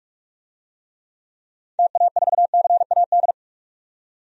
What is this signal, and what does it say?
Morse code, heard as a keyed tone, about 46 wpm, 700 Hz. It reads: TA4CAD
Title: DXCC-100 Call Signs - 6 Characters - 45wpm